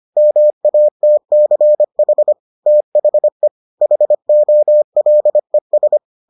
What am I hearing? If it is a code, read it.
Morse code, heard as a keyed tone, about 25 wpm, 600 Hz. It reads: MATCH THE HOLES